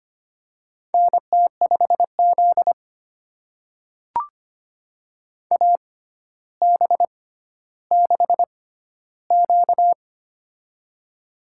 Morse code, heard as a keyed tone, about 25 wpm, spaced as for 13 wpm, 700 Hz. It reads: NT5Z AB6Q
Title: USA Call Signs - 4 Characters - 6x Character Spacing - 25wpm